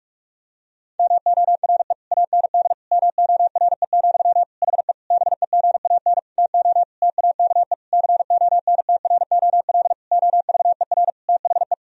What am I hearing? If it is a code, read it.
Morse code, heard as a keyed tone, about 41 wpm, 700 Hz. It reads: MORE AND MORE, HE BEGAN TO TAKE CONTROL OVER THE
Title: Sentences from Top 1000 Words - Review Part 3 - 45wpm